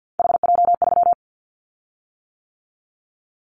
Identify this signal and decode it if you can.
Morse code, heard as a keyed tone, about 50 wpm, 700 Hz. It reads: HJ3
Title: Callsign Prefixes for Top 100 DXCC - 50wpm